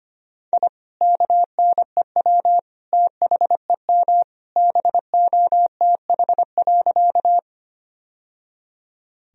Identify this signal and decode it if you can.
Morse code, heard as a keyed tone, about 25 wpm, 700 Hz. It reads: I KNEW THEM BOTH.
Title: Sentences from Top 1000 Words - Review Part 1 - 25wpm